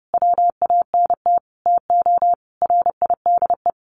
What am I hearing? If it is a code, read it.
Morse code, heard as a keyed tone, about 28 wpm, 700 Hz. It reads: WANT TO RIDE
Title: Sentences from Top 1200 Words 30wpm